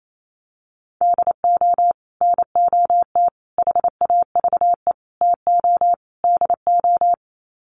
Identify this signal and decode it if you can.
Morse code, heard as a keyed tone, about 28 wpm, 700 Hz. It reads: DO NOT HAVE TO DO
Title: Sets of 5 Words 28wpm